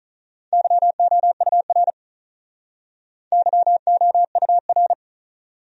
Morse code, heard as a keyed tone, about 44 wpm, 700 Hz. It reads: YOUR YOUR